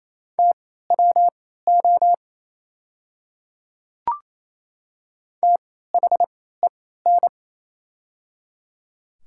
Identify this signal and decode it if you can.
Morse code, heard as a keyed tone, about 27 wpm, spaced as for 16 wpm, 700 Hz. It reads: TWO THEN
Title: 3x Character Spacing - Top 200 Words - 28wpm